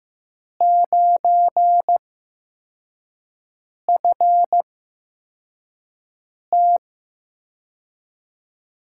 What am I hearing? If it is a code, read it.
Morse code, heard as a keyed tone, about 15 wpm, spaced as for 4 wpm, 700 Hz. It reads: 9FT
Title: USA Call Signs - 4 Characters - 8x Character Spacing - 15wpm